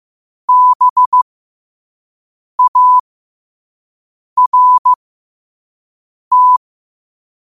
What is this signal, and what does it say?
Morse code, heard as a keyed tone, about 15 wpm, spaced as for 5 wpm, 1000 Hz. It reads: BART
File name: qotdfiles / 5wpm